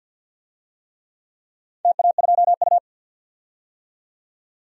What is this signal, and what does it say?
Morse code, heard as a keyed tone, about 51 wpm, 700 Hz. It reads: TA2U
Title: DXCC-100 Call Signs - 4 Characters - 50wpm